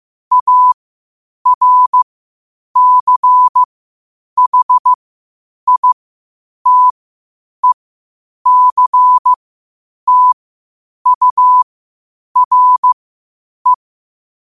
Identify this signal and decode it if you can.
Morse code, heard as a keyed tone, about 15 wpm, spaced as for 8 wpm, 1000 Hz. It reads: ARCHITECTURE
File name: qotdfiles / 8wpm